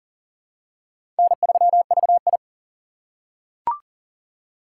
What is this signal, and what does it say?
Morse code, heard as a keyed tone, about 38 wpm, 700 Hz. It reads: N3VI
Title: USA Call Signs - 4 Characters - 6x Character Spacing - 40wpm